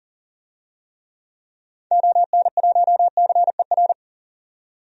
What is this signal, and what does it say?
Morse code, heard as a keyed tone, about 40 wpm, 700 Hz. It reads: ON1CER
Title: DXCC-100 Call Signs - 6 Characters - 40wpm